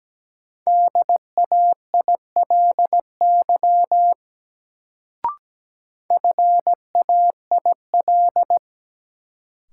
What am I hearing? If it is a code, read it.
Morse code, heard as a keyed tone, about 17 wpm, 700 Hz. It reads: DAILY FAIL
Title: Sentences from Top 1100 Words - New Word Drill Alt - 17wpm